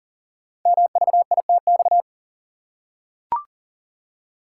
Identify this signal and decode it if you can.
Morse code, heard as a keyed tone, about 40 wpm, 690 Hz. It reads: MVITX
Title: Code Groups - 5 Letters - 40wpm